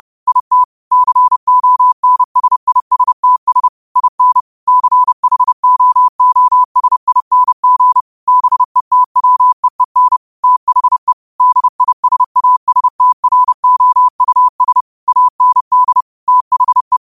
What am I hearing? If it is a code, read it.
Morse code, heard as a keyed tone, about 30 wpm, 1000 Hz. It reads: IT CONSISTS IN CHOOSING BETWEEN THE DISASTROUS AND THE